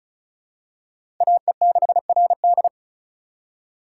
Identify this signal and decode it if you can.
Morse code, heard as a keyed tone, about 36 wpm, 700 Hz. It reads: AE6RD
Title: USA Call Signs - 8x Character Spacing - 35wpm